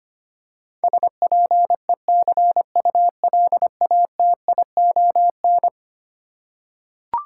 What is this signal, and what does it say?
Morse code, heard as a keyed tone, about 25 wpm, 700 Hz. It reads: SPECULATION
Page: morse-code-ninja-practice-25wpm